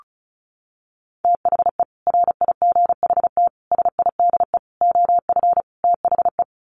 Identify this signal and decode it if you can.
Morse code, heard as a keyed tone, about 34 wpm, 700 Hz. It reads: THE RIGHT SIDE OF THE
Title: Sets of 5 Words Encore 35wpm